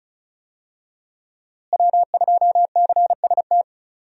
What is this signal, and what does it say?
Morse code, heard as a keyed tone, about 34 wpm, 700 Hz. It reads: W2CST
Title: USA Call Signs - 5 Characters - Repeated 2x - 35wpm